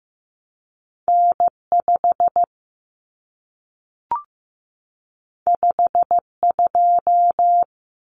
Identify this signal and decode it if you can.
Morse code, heard as a keyed tone, about 15 wpm, 700 Hz. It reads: N5 52
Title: Common Cut Numbers - 2-Digit 15wpm